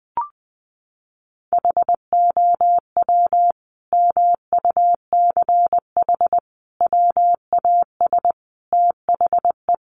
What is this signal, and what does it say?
Morse code, heard as a keyed tone, about 20 wpm, 700 Hz. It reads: HOW MUCH WAS THE
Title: Sentences from Top 1400 Words 20wpm